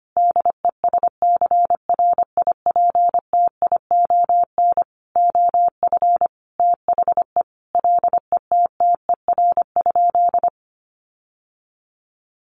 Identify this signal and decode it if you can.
Morse code, heard as a keyed tone, about 25 wpm, 700 Hz. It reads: DESCRIPTION OF THE LETTER?
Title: Sentences from Top 1300 Words 25wpm